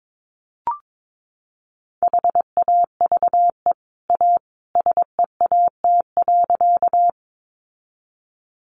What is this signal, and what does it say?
Morse code, heard as a keyed tone, about 22 wpm, 700 Hz. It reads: HAVE A SEAT.
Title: Sentences from Top 1500 Words 22wpm